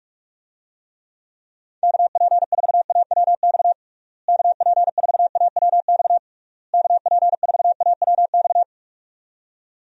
Morse code, heard as a keyed tone, about 41 wpm, 700 Hz. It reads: KP4AWX KP4AWX KP4AWX
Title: DXCC-100 Call Signs - 6 Characters - Repeated 3x - 45wpm